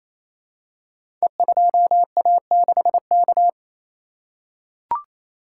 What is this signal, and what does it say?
Morse code, heard as a keyed tone, about 28 wpm, 700 Hz. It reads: E2A6K
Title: Code Groups - 5 Letter:Numbers - 28wpm